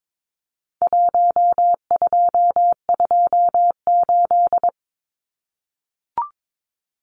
Morse code, heard as a keyed tone, about 22 wpm, 700 Hz. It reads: 1228
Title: Numbers - Contest Style - 4-Digit 22wpm